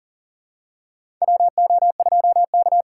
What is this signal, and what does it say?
Morse code, heard as a keyed tone, about 40 wpm, 690 Hz. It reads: WO2K
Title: USA Call Signs - 4x Character Spacing - 40wpm